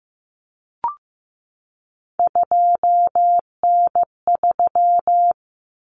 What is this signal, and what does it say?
Morse code, heard as a keyed tone, about 15 wpm, 700 Hz. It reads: 2N3
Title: Common Cut Numbers - 3-Digit 15wpm